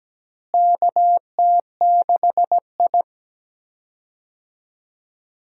Morse code, heard as a keyed tone, about 17 wpm, 700 Hz. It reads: KT6I
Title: USA Call Signs - 4 Characters - Repeated 3x - 17wpm